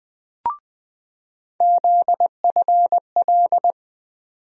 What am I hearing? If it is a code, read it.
Morse code, heard as a keyed tone, about 20 wpm, 700 Hz. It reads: ZFL
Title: USA Call Sign Postfixes - 3 Letters 20wpm